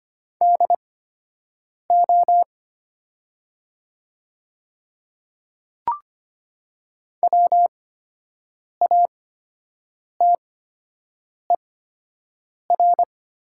Morse code, heard as a keyed tone, about 25 wpm, spaced as for 7 wpm, 700 Hz. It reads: DO WATER
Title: 8x Character Spacing - Top 200 Words - 25wpm